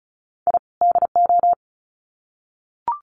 Morse code, heard as a keyed tone, about 34 wpm, 700 Hz. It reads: I DO
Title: Sets of 2 Words 35wpm